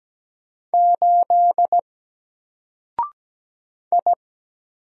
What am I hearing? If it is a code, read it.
Morse code, heard as a keyed tone, about 17 wpm, 700 Hz. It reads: 8 I
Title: Single Letter-Number Alt 17wpm